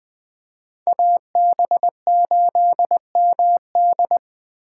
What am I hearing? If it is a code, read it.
Morse code, heard as a keyed tone, about 20 wpm, 700 Hz. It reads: AB8MD
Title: Speed Racing - USA 2x2 Call Signs 17wpm